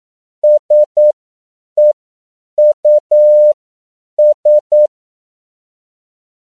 Morse code, heard as a keyed tone, about 9 wpm, spaced as for 7 wpm, 600 Hz. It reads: SEUS
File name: CW / 10PPM